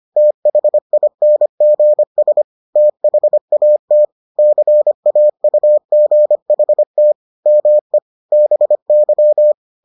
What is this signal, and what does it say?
Morse code, heard as a keyed tone, about 25 wpm, 600 Hz. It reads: THINGS THAT CAUGHT ME BY